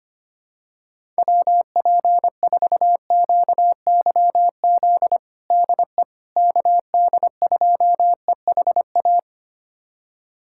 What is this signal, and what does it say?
Morse code, heard as a keyed tone, about 25 wpm, 700 Hz. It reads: WP4QYZ DE KD2EHA
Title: USA Callsign DE Callsign - 6 Characters - 25wpm